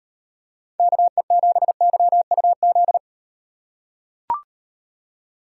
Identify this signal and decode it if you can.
Morse code, heard as a keyed tone, about 35 wpm, 700 Hz. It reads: KE7YUZ